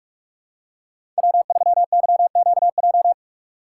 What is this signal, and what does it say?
Morse code, heard as a keyed tone, about 44 wpm, 700 Hz. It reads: W3YQJ
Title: Code Groups - 5 Letter:Numbers - 45wpm